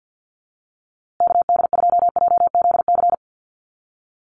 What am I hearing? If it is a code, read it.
Morse code, heard as a keyed tone, about 50 wpm, 700 Hz. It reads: KD2JZC
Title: USA Call Signs Repeated 2x 50wpm